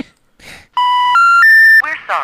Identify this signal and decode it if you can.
Special information tone, with speech.